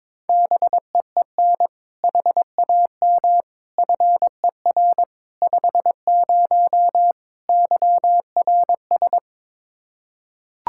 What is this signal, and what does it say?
Morse code, heard as a keyed tone, about 22 wpm, 700 Hz. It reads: BEEN HAM FER 50 YRS
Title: Practice QSO - Exchange 3 - More Info - 22wpm